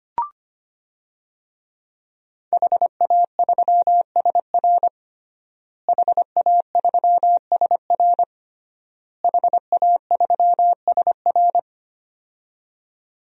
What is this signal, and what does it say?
Morse code, heard as a keyed tone, about 25 wpm, 700 Hz. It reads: HA3SR HA3SR HA3SR
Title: DXCC-100 Call Signs - 5 Characters - Repeated 3x - 25wpm